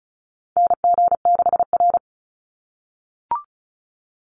Morse code, heard as a keyed tone, about 34 wpm, 700 Hz. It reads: NG6R
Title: NA SOTA Chaser Callsigns 35wpm